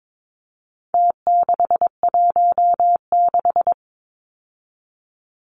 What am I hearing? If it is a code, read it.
Morse code, heard as a keyed tone, about 22 wpm, 700 Hz. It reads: T616
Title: Numbers - Contest Style - 4-Digit 22wpm